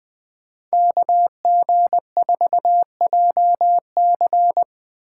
Morse code, heard as a keyed tone, about 20 wpm, 700 Hz. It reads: KG4JC
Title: Speed Racing - USA 2x2 Call Signs 17wpm